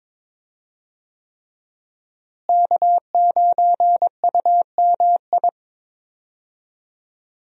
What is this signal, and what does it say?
Morse code, heard as a keyed tone, about 22 wpm, 700 Hz. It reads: K9UMI
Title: USA Call Signs - 5 Characters - Repeated 3x - 22wpm